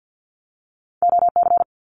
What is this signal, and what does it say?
Morse code, heard as a keyed tone, about 50 wpm, 700 Hz. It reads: OC
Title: Abbreviations Repeated 3x - 50wpm